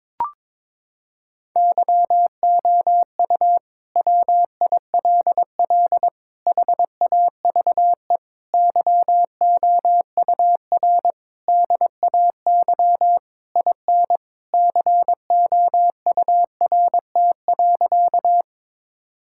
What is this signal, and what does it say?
Morse code, heard as a keyed tone, about 22 wpm, 700 Hz. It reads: YOU WILL HAVE YOUR DAY IN COURT.
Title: Sentences from Top 1600 Words 22wpm